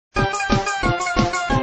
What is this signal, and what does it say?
There is no code.